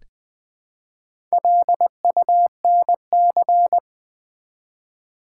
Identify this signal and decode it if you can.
Morse code, heard as a keyed tone, about 20 wpm, 700 Hz. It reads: LUNC
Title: Build A Word - 1 Letter At A Time - 5-Letter Words - 20wpm